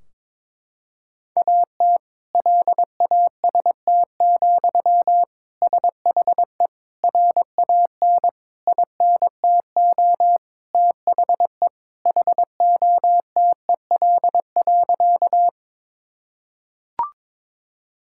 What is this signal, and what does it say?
Morse code, heard as a keyed tone, about 22 wpm, 700 Hz. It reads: AT LAST, SHE RAN INTO THE HOTEL.
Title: Sentences from Top 1600 Words 22wpm